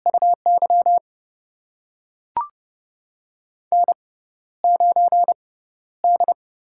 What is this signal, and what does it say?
Morse code, heard as a keyed tone, about 30 wpm, spaced as for 13 wpm, 700 Hz. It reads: UY N9D
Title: USA Call Signs - 4 Characters - 6x Character Spacing - 30wpm